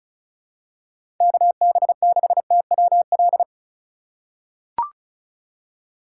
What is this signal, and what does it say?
Morse code, heard as a keyed tone, about 36 wpm, 700 Hz. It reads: KB6TWL
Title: USA Call Signs - 6 Characters - 2x Character Spacing - 35wpm